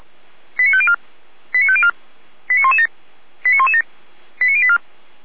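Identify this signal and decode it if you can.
Five-tone selective call (ZVEI), heard as quick runs of five tones, tones of about 70 ms. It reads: 89504 89504 89117 89117 89084
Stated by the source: ZVEI five-tone selective call received on SDR.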